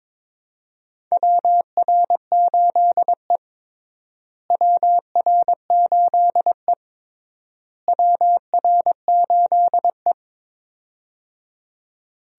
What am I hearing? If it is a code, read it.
Morse code, heard as a keyed tone, about 21 wpm, 700 Hz. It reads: WR8E WR8E WR8E
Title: USA Call Signs - 4 Characters - Repeated 3x - 22wpm